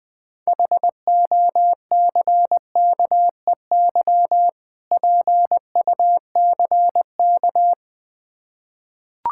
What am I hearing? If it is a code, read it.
Morse code, heard as a keyed tone, about 21 wpm, 700 Hz. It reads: HOCKEY PUCK
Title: Difficult Words with Hints - 20wpm